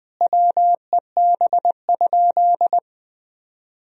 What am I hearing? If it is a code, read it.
Morse code, heard as a keyed tone, about 21 wpm, 700 Hz. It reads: WEB?
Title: Sentences from Top 1000 Words - Review Part 4 - 20wpm